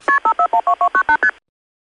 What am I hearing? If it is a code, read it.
Multi-frequency trunk signalling, heard as fast pairs of tones, tones of about 90 ms. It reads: KP5412298ST